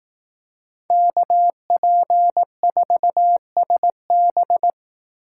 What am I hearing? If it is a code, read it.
Morse code, heard as a keyed tone, about 18 wpm, 700 Hz. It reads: KP4SB
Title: Speed Racing - USA 2x2 Call Signs 15wpm